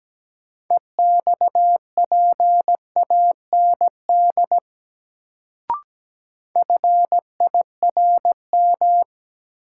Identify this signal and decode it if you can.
Morse code, heard as a keyed tone, about 17 wpm, 700 Hz. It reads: EXPAND FIRM
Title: Sentences from Top 1400 Words - New Word Drill - 17wpm